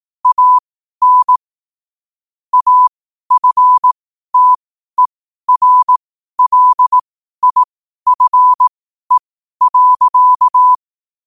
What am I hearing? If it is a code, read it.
Morse code, heard as a keyed tone, about 18 wpm, spaced as for 12 wpm, 1000 Hz. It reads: AN AFTERLIFE.
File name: qotdfiles / 12wpm